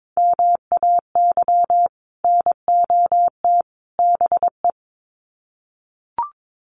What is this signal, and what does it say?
Morse code, heard as a keyed tone, about 22 wpm, 700 Hz. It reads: MAY NOT BE
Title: Sets of 3 Words 22wpm